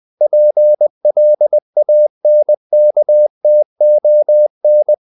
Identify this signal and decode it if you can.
Morse code, heard as a keyed tone, about 20 wpm, 600 Hz. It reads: PLANKTON
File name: the old man and the sea / the old man and the sea part 1 20wpm